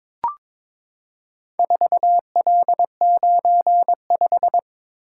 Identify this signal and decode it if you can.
Morse code, heard as a keyed tone, about 22 wpm, 700 Hz. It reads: 4L95
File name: practice / 22wpm / Four Letter-Number Combinations - 22wpm